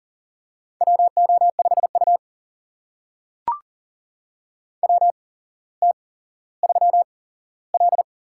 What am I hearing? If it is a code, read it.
Morse code, heard as a keyed tone, about 40 wpm, spaced as for 17 wpm, 700 Hz. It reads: WO5U WT3L
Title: USA Call Signs - 4 Characters - 8x Character Spacing - 40wpm